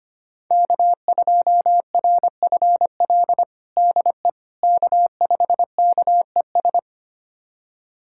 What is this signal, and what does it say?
Morse code, heard as a keyed tone, about 25 wpm, 700 Hz. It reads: K2RFL DE K5KES
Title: USA Callsign DE Callsign - 5 Characters - 25wpm